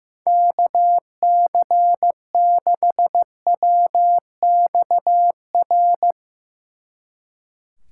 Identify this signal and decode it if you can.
Morse code, heard as a keyed tone, about 16 wpm, 700 Hz. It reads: KC6WXR
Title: Speed Racing - USA Call Signs 15wpm